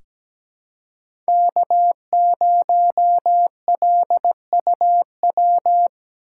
Morse code, heard as a keyed tone, about 17 wpm, 700 Hz. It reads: K0LUW
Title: K1USN SST QSO Practice - 17wpm